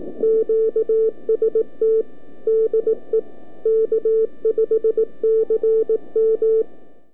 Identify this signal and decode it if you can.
Morse code, heard as a keyed tone, about 18 wpm, 440 Hz. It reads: QST DE K5CM
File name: k5cm-40mCallup-by-abi